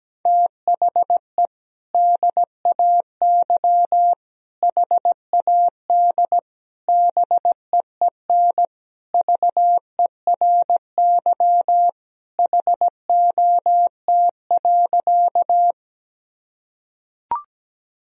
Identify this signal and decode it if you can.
Morse code, heard as a keyed tone, about 17 wpm, 700 Hz. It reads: THE DAY HAD BEEN VERY HOT.
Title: Sentences from Top 1000 Words - Review Part 3 - 17wpm